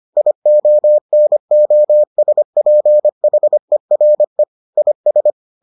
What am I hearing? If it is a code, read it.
Morse code, heard as a keyed tone, about 25 wpm, 600 Hz. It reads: IONOSPHERE IS